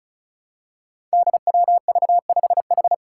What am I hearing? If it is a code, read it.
Morse code, heard as a keyed tone, about 34 wpm, 700 Hz. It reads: DWV5H
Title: Five Challenging Letter-Number Combinations 35wpm